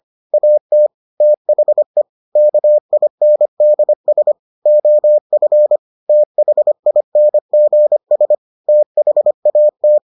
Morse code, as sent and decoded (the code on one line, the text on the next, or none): .- - / - .... / -.- .. -. -.. ... / --- ..-. / - .... .. -. --. ... / - .... .- -
AT THE KINDS OF THINGS THAT